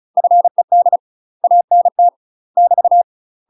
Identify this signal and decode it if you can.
Morse code, heard as a keyed tone, about 36 wpm, 700 Hz. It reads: FED ANT <BT>
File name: UK MASTER QSOs TRAINING / 35WPM / UK MASTER QSOs 0016